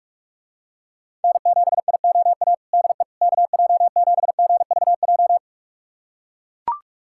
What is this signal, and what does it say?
Morse code, heard as a keyed tone, about 43 wpm, 700 Hz. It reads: N7IOA DE KJ7GVJ
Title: USA Callsign DE Callsign 45wpm